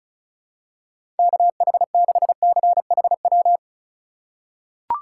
Morse code, heard as a keyed tone, about 35 wpm, 690 Hz. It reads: KH6CHW
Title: USA Call Signs - 6 Characters - 6x Character Spacing - 35wpm